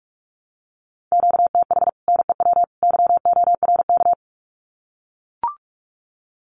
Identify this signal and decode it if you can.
Morse code, heard as a keyed tone, about 44 wpm, 700 Hz. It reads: QTH NEW YORK
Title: Common QSO Phrases 45wpm